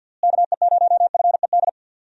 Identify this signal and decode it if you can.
Morse code, heard as a keyed tone, about 52 wpm, 700 Hz. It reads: KE0FED
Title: USA Call Signs - 6 Characters - Repeated 3x - 50wpm